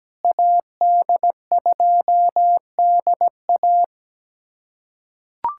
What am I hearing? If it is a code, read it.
Morse code, heard as a keyed tone, about 17 wpm, 700 Hz. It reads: AD2DA
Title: USA Callsign DE Callsign - 5 Characters - 17wpm